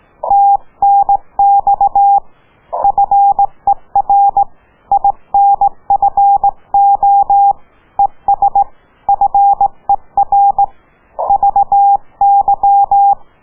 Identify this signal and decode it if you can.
Morse code, heard as a keyed tone, about 18 wpm, 800 Hz. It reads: TNX FER INFO ES FER VY